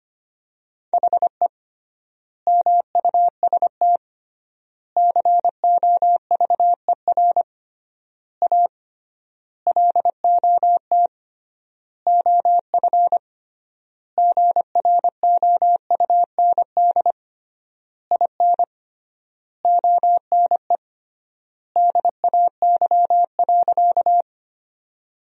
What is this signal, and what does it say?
Morse code, heard as a keyed tone, about 25 wpm, 700 Hz. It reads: HE MUST COVER A LOT OF GROUND IN ONE DAY.
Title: 3x Word-Spacing - Sentences from Top 500 Words - 25wpm